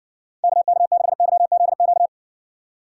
Morse code, heard as a keyed tone, about 59 wpm, 700 Hz. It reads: KK6YZQ